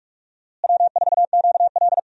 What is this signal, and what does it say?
Morse code, heard as a keyed tone, about 44 wpm, 700 Hz. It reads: W4QL
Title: USA Call Signs - 4 Characters - Repeated 3x - 45wpm